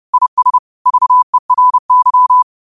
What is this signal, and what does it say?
Morse code, heard as a keyed tone, about 30 wpm, 1000 Hz. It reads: IS VERY
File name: qotdfiles / 30wpm